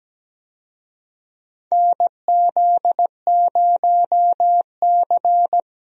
Morse code, heard as a keyed tone, about 17 wpm, 700 Hz. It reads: NZ0C
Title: USA Call Signs - 4 Characters - Repeated 2x - 17wpm